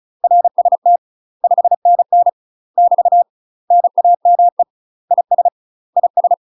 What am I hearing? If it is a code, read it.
Morse code, heard as a keyed tone, about 35 wpm, 700 Hz. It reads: RST 5NN <BT> NAME IS IS
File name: UK MASTER QSOs TRAINING / 35WPM / UK MASTER QSOs 0011